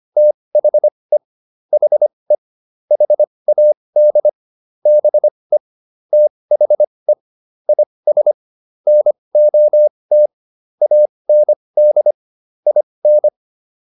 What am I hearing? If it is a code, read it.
Morse code, heard as a keyed tone, about 25 wpm, spaced as for 20 wpm, 600 Hz. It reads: THE HE HAD BE THE IS NOT AND IN